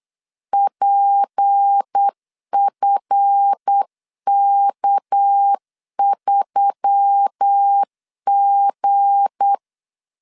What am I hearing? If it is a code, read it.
Morse code, heard as a keyed tone, about 8 wpm, 800 Hz. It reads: PFK3G
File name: LEZIONI / 7 GF9Z / 7 GF 8wpm 3108